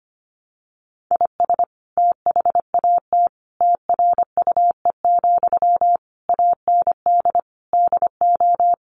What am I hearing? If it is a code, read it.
Morse code, heard as a keyed tone, about 25 wpm, 700 Hz. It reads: IS THAT TRUE, AND DO
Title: Sentences from Top 1200 Words 25wpm